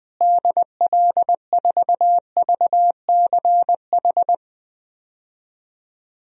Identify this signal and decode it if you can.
Morse code, heard as a keyed tone, about 21 wpm, 700 Hz. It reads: DL4VCH
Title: DXCC-100 Call Signs - 6 Characters - Repeated 2x - 20wpm